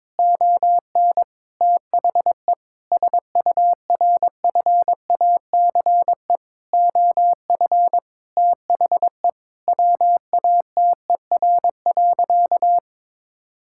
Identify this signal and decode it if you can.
Morse code, heard as a keyed tone, about 22 wpm, 700 Hz. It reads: ON THE SURFACE OF THE WATER.
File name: practice / 22wpm / Sentences with 4-Word Phrases - 22wpm